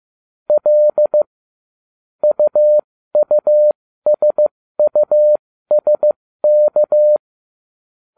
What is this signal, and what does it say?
Morse code, heard as a keyed tone, about 15 wpm, spaced as for 13 wpm, 600 Hz. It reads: L UUSUSK
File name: cw / ubacw09 1